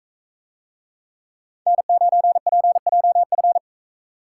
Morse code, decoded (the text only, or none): N9PJF